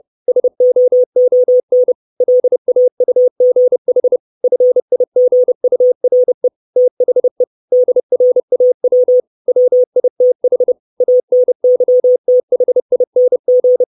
Morse code, heard as a keyed tone, about 30 wpm, 500 Hz. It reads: SOON LAUGH FIGURE THE DRAW WITH ANYTHING
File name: 632 Words 30WPM